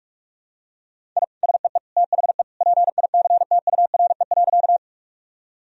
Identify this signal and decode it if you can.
Morse code, heard as a keyed tone, about 46 wpm, 700 Hz. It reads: I SEE THE PICTURE.
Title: Sentences from Top 1000 Words - Review Part 1 - 45wpm